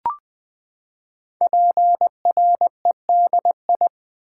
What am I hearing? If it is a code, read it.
Morse code, heard as a keyed tone, about 20 wpm, 700 Hz. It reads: PREDI
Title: Build A Word - 5 Letters At A Time - 10-Letter Words - 20wpm